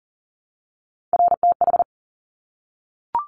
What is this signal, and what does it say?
Morse code, heard as a keyed tone, about 38 wpm, 700 Hz. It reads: RTH
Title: Top Three Letter Combinations in Words 40wpm